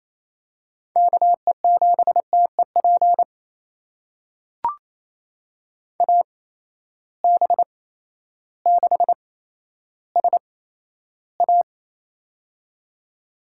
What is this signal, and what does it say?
Morse code, heard as a keyed tone, about 28 wpm, spaced as for 13 wpm, 700 Hz. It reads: KE7TEP AB6SA
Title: USA Call Signs - 8x Character Spacing - 28wpm